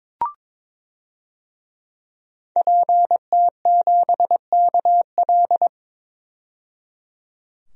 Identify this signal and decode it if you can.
Morse code, heard as a keyed tone, about 22 wpm, 700 Hz. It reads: PT7KL
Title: DXCC-100 Call Signs - 5 Characters - 22wpm